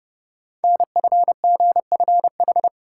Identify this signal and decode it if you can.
Morse code, heard as a keyed tone, about 30 wpm, 700 Hz. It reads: NFGFH
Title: Code Groups - 5 Letters - 30wpm